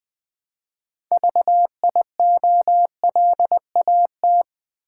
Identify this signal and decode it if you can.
Morse code, heard as a keyed tone, about 20 wpm, 700 Hz. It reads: VIOLAT
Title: Build A Word - 3 Letters At A Time - 9-Letter Words - 20wpm